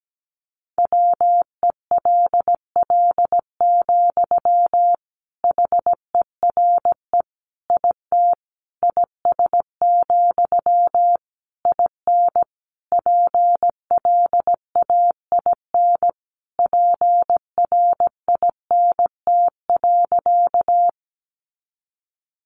Morse code, heard as a keyed tone, about 17 wpm, 700 Hz. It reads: WELL, HERE IT IS, IN PLAIN PRINT.
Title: Sentences from Top 1400 Words 17wpm